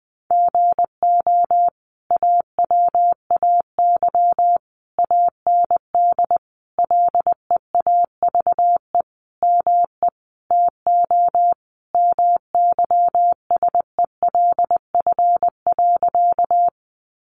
Morse code, heard as a keyed tone, about 20 wpm, 700 Hz. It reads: GO AWAY AND LEAVE ME TO MYSELF.
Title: Sentences from Top 1300 Words 20wpm